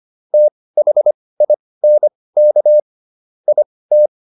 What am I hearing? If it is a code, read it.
Morse code, heard as a keyed tone, about 25 wpm, spaced as for 18 wpm, 600 Hz. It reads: THINK IT